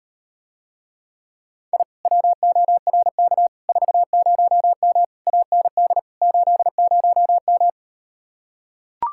Morse code, heard as a keyed tone, about 39 wpm, 700 Hz. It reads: I WORK 40M AND 80M